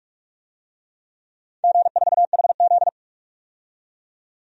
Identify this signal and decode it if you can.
Morse code, heard as a keyed tone, about 46 wpm, 700 Hz. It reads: G4HZ
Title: DXCC-100 Call Signs - 4 Characters - 45wpm